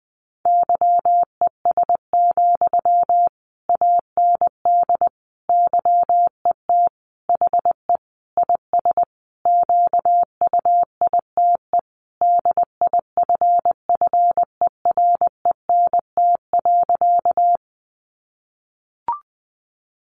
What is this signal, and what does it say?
Morse code, heard as a keyed tone, about 20 wpm, 700 Hz. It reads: YES, AND YET HE IS QUITE DIFFERENT.
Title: Sentences from Top 700 Words 20wpm